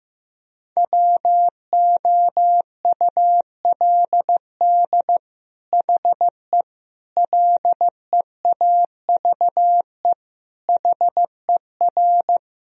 Morse code, heard as a keyed tone, about 15 wpm, 700 Hz. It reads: WOULD HE LEAVE HER
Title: Sentences from Top 1000 Words - Review Part 1 - 15wpm